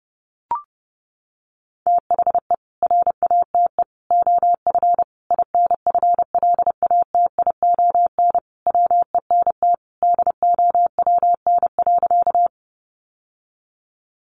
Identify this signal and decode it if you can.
Morse code, heard as a keyed tone, about 30 wpm, 700 Hz. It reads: THE RATE OF INFLATION WENT DOWN.
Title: Sentences from Top 1100 Words 30wpm